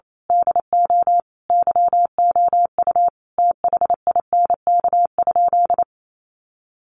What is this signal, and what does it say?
Morse code, heard as a keyed tone, about 28 wpm, 700 Hz. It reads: DO YOU THINK?
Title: Sentences from Top 200 Words 28wpm